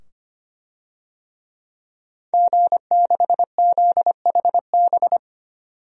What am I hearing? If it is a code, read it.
Morse code, heard as a keyed tone, about 25 wpm, 700 Hz. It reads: G6ZHB
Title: DXCC-100 Call Signs - 5 Characters - 25wpm